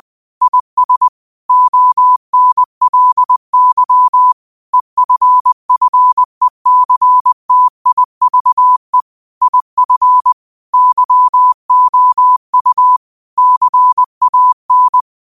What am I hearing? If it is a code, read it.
Morse code, heard as a keyed tone, about 20 wpm, 1000 Hz. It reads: IS ONLY EFFECTIVE IF YOU CAN